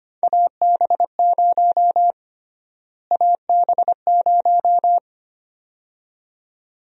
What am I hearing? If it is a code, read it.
Morse code, heard as a keyed tone, about 25 wpm, 700 Hz. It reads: AB0 AB0
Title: USA Call Sign Prefixes - Repeated 2x - 25wpm